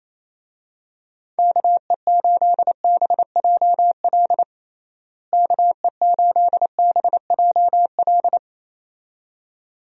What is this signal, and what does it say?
Morse code, heard as a keyed tone, about 28 wpm, 700 Hz. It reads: KE8BJL KE8BJL
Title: USA Call Signs - 6 Characters - Repeated 2x - 28wpm